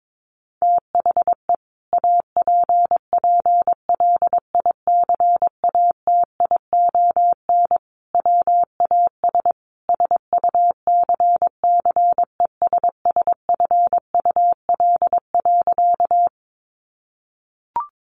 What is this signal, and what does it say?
Morse code, heard as a keyed tone, about 22 wpm, 700 Hz. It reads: THE APPLICATION WAS SUCCESSFUL.
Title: Sentences from Top 1200 Words 22wpm